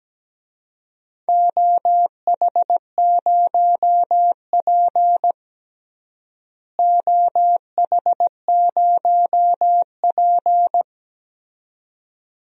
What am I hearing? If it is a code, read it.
Morse code, heard as a keyed tone, about 17 wpm, 700 Hz. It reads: OH0P OH0P
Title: DXCC-100 Call Signs - 4 Characters - Repeated 2x - 17wpm